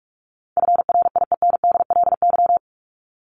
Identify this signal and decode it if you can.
Morse code, heard as a keyed tone, about 44 wpm, 700 Hz. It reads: FRIENDLY
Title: Top 1751-2000 Words 45wpm